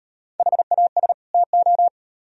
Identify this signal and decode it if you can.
Morse code, heard as a keyed tone, about 38 wpm, 700 Hz. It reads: HAS TO